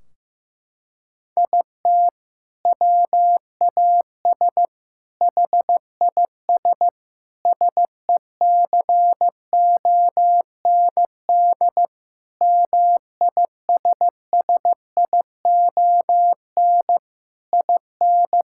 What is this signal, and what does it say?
Morse code, heard as a keyed tone, about 15 wpm, 700 Hz. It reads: IT WAS HIS SECOND MISSION IN
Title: Sentences from Top 1600 Words 15wpm